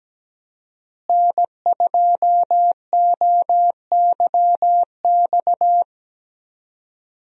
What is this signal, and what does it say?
Morse code, heard as a keyed tone, about 17 wpm, 700 Hz. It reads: N2OYX
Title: USA Call Signs - 5 Characters - Repeated 2x - 17wpm